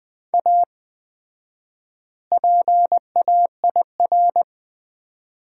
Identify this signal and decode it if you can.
Morse code, heard as a keyed tone, about 20 wpm, 700 Hz. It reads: A PAIR
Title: 4x Word-Spacing - Sets of 3 Words - 20wpm